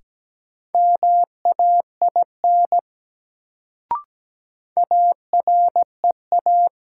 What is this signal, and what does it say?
Morse code, heard as a keyed tone, about 17 wpm, 700 Hz. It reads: MAIN AREA
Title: Sentences from Top 1000 Words - Review Part 2 - New Word Drill - 17wpm